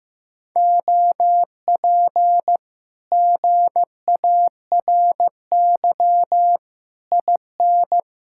Morse code, heard as a keyed tone, about 15 wpm, 700 Hz. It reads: OP GARY IN
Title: Speed-Racing Common QSO Phrases Encore - 15wpm